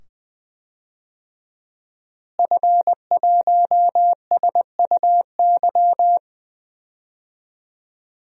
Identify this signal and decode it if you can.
Morse code, heard as a keyed tone, about 20 wpm, 700 Hz. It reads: F1SUY
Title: DXCC-100 Call Signs - 5 Characters - Repeated 2x - 20wpm